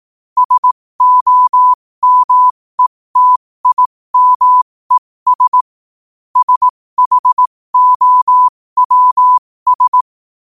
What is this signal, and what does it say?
Morse code, heard as a keyed tone, about 18 wpm, spaced as for 15 wpm, 1000 Hz. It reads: SOMETIMES SHOWS
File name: qotdfiles / 15wpm